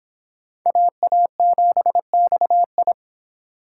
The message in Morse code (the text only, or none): AA7XI